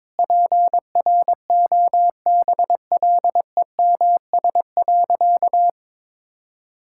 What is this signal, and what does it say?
Morse code, heard as a keyed tone, about 22 wpm, 700 Hz. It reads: PROBLEMS.
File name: practice / 22wpm / Easy Sentences with Hints - 22wpm